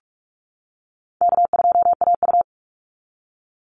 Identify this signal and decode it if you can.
Morse code, heard as a keyed tone, about 46 wpm, 700 Hz. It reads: K2AU
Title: Easy Call Signs - 45wpm